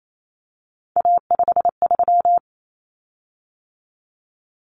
Morse code, heard as a keyed tone, about 28 wpm, 700 Hz. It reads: A53
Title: Common Cut Numbers - 3-Digit 28wpm